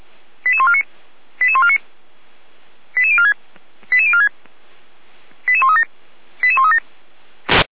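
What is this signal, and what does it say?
Five-tone selective call (ZVEI), heard as quick runs of five tones, tones of about 70 ms. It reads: 88149 88149 88046 88046 88137 88137
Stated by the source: ZVEI five-tone selective call 88347, received by SDR.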